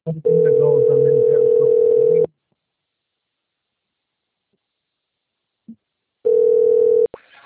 Ringback tone (North America), 440 + 480 Hz.
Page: Limeira IV